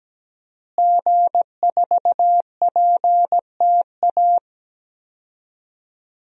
Morse code, heard as a keyed tone, about 17 wpm, 700 Hz. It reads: G4PTA
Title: DXCC-100 Call Signs - 5 Characters - Repeated 3x - 17wpm